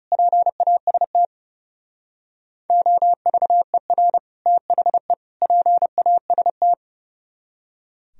Morse code, heard as a keyed tone, about 35 wpm, 700 Hz. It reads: PAST OVER THE PAST
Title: Speed-Racing - Sets of 3 Words 30wpm